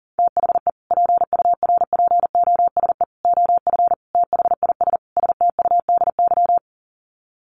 Morse code, heard as a keyed tone, about 40 wpm, 700 Hz. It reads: THE PURPOSE OF THIS STUDY
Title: Sets of 5 Words 40wpm